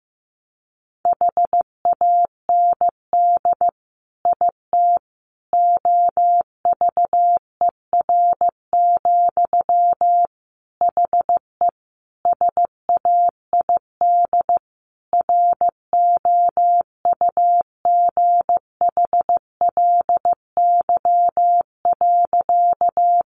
Morse code, heard as a keyed tone, about 15 wpm, 700 Hz. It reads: HAND IT OVER, HE SAID ROUGHLY.
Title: Sentences from Top 1500 Words 15wpm